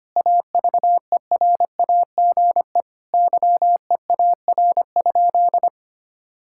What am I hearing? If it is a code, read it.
Morse code, heard as a keyed tone, about 25 wpm, 700 Hz. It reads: AVERAGE YEAR?
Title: Sentences from Top 1000 Words - Review Part 6 - 25wpm